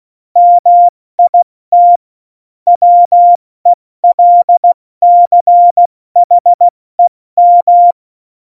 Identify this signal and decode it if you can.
Morse code, heard as a keyed tone, about 16 wpm, 700 Hz. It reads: MIT WELCHEM
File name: Morsekurs / Witze / Witz 18 Juli 17, Tempo 70 F80